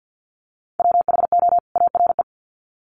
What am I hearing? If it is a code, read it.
Morse code, heard as a keyed tone, about 48 wpm, 700 Hz. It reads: WHO ARE
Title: Sets of 2 Words 50wpm